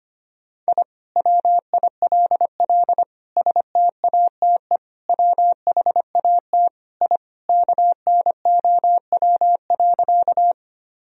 Morse code, heard as a keyed tone, about 25 wpm, 700 Hz. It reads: I WILL STATE WHAT I KNOW.
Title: Sentences from Top 1000 Words - Review Part 2 - 25wpm